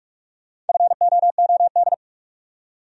Morse code, heard as a keyed tone, about 43 wpm, 700 Hz. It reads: FOOD